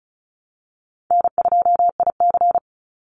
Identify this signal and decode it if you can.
Morse code, heard as a keyed tone, about 34 wpm, 700 Hz. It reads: N2IC
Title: Easy Call Signs Repeated 3x - 35wpm